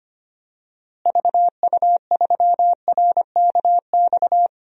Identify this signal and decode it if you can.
Morse code, heard as a keyed tone, about 25 wpm, 700 Hz. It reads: VU3RKX
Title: DXCC-100 Call Signs - 6 Characters - Repeated 3x - 25wpm